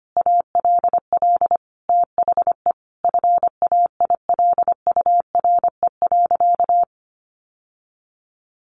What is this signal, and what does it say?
Morse code, heard as a keyed tone, about 25 wpm, 700 Hz. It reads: ALL THE FAILURE.
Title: Sentences from Top 900 Words 25wpm